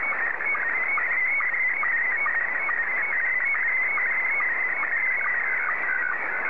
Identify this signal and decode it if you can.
SSTV (slow-scan television), Scottie 1.